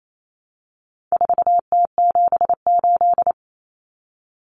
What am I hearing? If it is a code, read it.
Morse code, heard as a keyed tone, about 28 wpm, 700 Hz. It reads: 4T78
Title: Common Cut Numbers - 4-Digit 28wpm